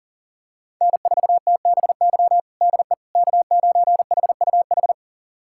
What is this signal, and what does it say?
Morse code, heard as a keyed tone, about 40 wpm, 700 Hz. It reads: N4TBY DE K9HUH
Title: USA Callsign DE Callsign - 5 Characters - 40wpm